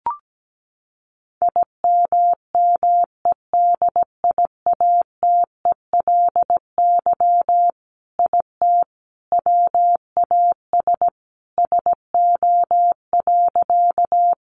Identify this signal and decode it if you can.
Morse code, heard as a keyed tone, about 17 wpm, 700 Hz. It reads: IMMEDIATELY IT WAS SO.
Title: Sentences from Top 1100 Words 17wpm